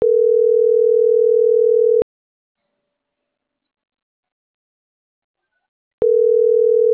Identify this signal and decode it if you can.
Ringback tone (North America), 440 + 480 Hz.
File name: CENTRAL MAYAB / out atzhyry.gomez - .82052